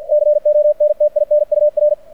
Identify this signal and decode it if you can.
Morse code, heard as a keyed tone, about 49 wpm, 600 Hz. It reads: CONTINUR
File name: archived / CRSnet